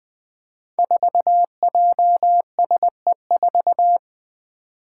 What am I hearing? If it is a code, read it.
Morse code, heard as a keyed tone, about 20 wpm, 700 Hz. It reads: 4JSE4
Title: Five Challenging Letter-Number Combinations 20wpm